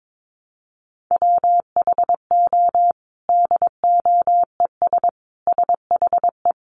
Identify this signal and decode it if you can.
Morse code, heard as a keyed tone, about 22 wpm, 700 Hz. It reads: WHO DOES SHE
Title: Sentences from Top 1300 Words 22wpm